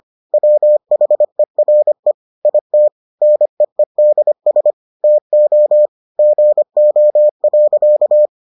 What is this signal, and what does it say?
Morse code, heard as a keyed tone, about 25 wpm, 600 Hz. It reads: WHERE IT NEEDS TO GO.